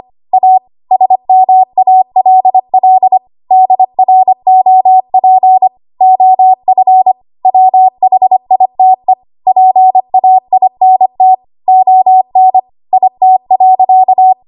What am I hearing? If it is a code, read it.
Morse code, heard as a keyed tone, about 25 wpm, 750 Hz. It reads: A SMALL DROP OF WHITE PAINT ON IT.
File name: Archive / 25 WPM / 25WPM